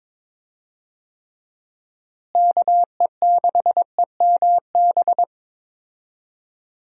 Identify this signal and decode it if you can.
Morse code, heard as a keyed tone, about 22 wpm, 700 Hz. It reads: KE6EMB
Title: USA Call Signs - 6 Characters - Repeated 2x - 22wpm